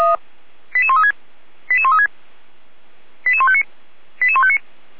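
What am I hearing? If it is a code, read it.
Five-tone selective call (ZVEI), heard as quick runs of five tones, tones of about 70 ms. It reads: 88136 88136 88159 88159
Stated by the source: Five-tone ZVEI selective call, SDR receiver, 455 channel, feuerwehr-orlamuende fire brigade.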